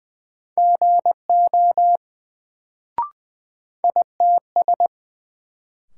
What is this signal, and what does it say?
Morse code, heard as a keyed tone, about 20 wpm, 700 Hz. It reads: GO ITS
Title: Sets of 2 Words - New Word Drill - 20wpm